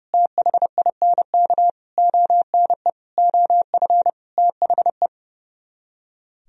Morse code, heard as a keyed tone, about 38 wpm, 700 Hz. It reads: THINK ONE OF THE